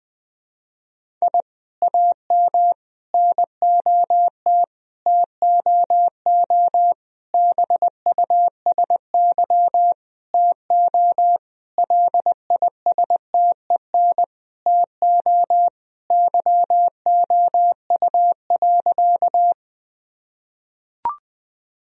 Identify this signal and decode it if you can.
Morse code, heard as a keyed tone, about 21 wpm, 700 Hz. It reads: I AM NOT TOO BUSY TO LISTEN TO YOU.
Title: Sentences from Top 1000 Words - Review Part 3 - 20wpm